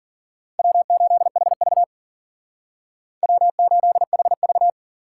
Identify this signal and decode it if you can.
Morse code, heard as a keyed tone, about 43 wpm, 700 Hz. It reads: W8HV W8HV
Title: Speed Racing - USA 1x2 Call Signs 40wpm